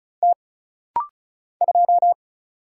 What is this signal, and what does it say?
Morse code, heard as a keyed tone, about 36 wpm, 700 Hz. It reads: T 2
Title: Single Letter-Number - Rapid-Fire 35wpm